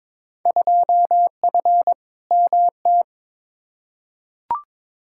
Morse code, heard as a keyed tone, about 22 wpm, 700 Hz. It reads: 2F MT
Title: Field Day Exchanges Encore - 22wpm